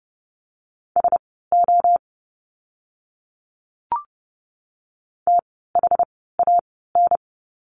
Morse code, heard as a keyed tone, about 30 wpm, spaced as for 17 wpm, 700 Hz. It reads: SO THAN
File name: course / 30:10wpm / 230 Lesson 37f - Letter x - Words 30wpm